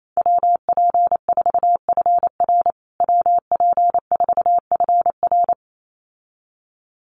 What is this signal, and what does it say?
Morse code, heard as a keyed tone, about 28 wpm, 700 Hz. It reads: WP4FR WP4FR
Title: USA Call Signs Repeated 2x 28wpm